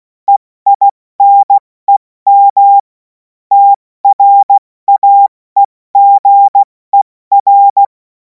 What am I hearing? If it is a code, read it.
Morse code, heard as a keyed tone, about 16 wpm, 800 Hz. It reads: EINEM TRAEGER
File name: Morsekurs / Witze / Witz 07. April 2026 70 F80